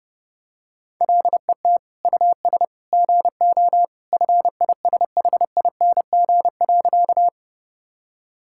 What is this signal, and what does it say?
Morse code, heard as a keyed tone, about 30 wpm, 700 Hz. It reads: LET US GO FISHING.
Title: Sentences from Top 1000 Words - Review Part 5 - 30wpm